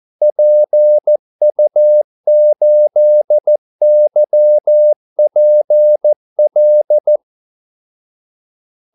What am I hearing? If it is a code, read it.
Morse code, heard as a keyed tone, about 14 wpm, 600 Hz. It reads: PU8YPL
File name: Callsigns-14wpm